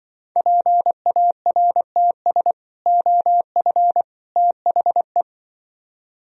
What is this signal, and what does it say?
Morse code, heard as a keyed tone, about 24 wpm, 700 Hz. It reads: PARTS OF THE
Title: Speed-Racing - Sets of 3 Words 20wpm